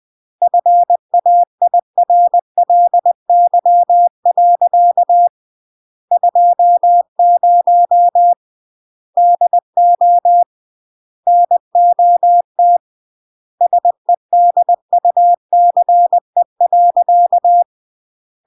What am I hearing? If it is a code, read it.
Morse code, heard as a keyed tone, about 20 wpm, 700 Hz. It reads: FAIRLY. 20 DO NOT SEDUCE.